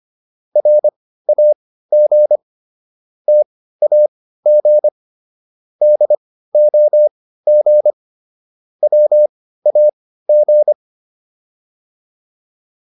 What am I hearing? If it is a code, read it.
Morse code, heard as a keyed tone, about 25 wpm, spaced as for 15 wpm, 600 Hz. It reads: RAG TAG DOG WAG
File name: PR302 15